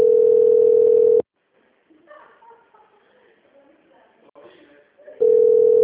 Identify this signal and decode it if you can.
Ringback tone (North America), 440 + 480 Hz, with speech.
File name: out- -2005- - - .139